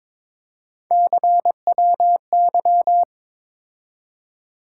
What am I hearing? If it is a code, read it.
Morse code, heard as a keyed tone, about 22 wpm, 700 Hz. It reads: CWY
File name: practice / 22wpm / Three Letter-Number Combinations - 22wpm